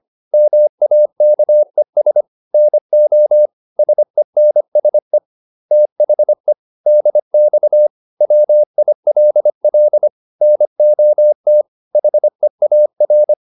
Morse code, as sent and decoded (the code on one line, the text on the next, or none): -- .- -.- . ... / -. --- / ... -. ... / - .... / -.. -..- / .-- .. .-.. .-.. / -. --- - / .... .- .-.
MAKES NO SENSE THE DX WILL NOT HEAR